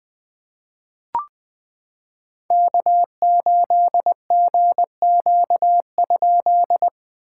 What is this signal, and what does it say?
Morse code, heard as a keyed tone, about 20 wpm, 700 Hz. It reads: K8GQ?
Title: POTA QSO Practice - 20wpm